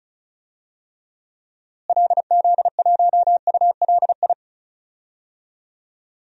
Morse code, heard as a keyed tone, about 34 wpm, 700 Hz. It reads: LZ1ULI
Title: DXCC-100 Call Signs - 6 Characters - Repeated 2x - 35wpm